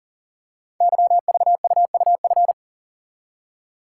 Morse code, heard as a keyed tone, about 38 wpm, 700 Hz. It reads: YVUUF
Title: Code Groups - 5 Letter:Numbers - 40wpm